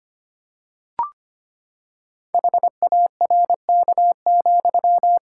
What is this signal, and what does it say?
Morse code, heard as a keyed tone, about 25 wpm, 700 Hz. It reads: HARK,
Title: Pangram Challenge v2 - 25wpm